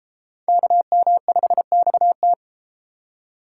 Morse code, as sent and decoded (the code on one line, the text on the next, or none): -.- -- ..... -..- -
KM5XT